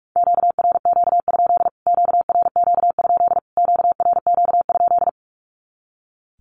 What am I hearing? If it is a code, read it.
Morse code, heard as a keyed tone, about 44 wpm, 700 Hz. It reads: QRQ? QRQ? QRQ?
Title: Q-Codes Repeated 3x 45wpm